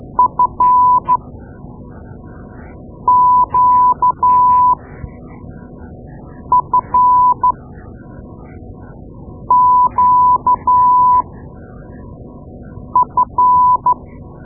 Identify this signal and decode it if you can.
Morse code, heard as a keyed tone, about 12 wpm, spaced as for 4 wpm, 1000 Hz. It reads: FQFQF
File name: Morse Lessons / Morse Lessons / Tape-2A